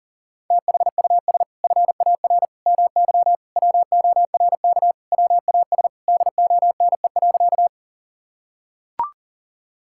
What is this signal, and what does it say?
Morse code, heard as a keyed tone, about 37 wpm, 700 Hz. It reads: THUS FAR MY WORK WAS DONE.
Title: Sentences from Top 1000 Words - Review Part 5 - 40wpm